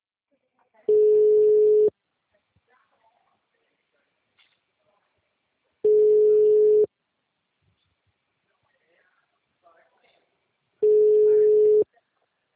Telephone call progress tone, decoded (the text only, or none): ringback tone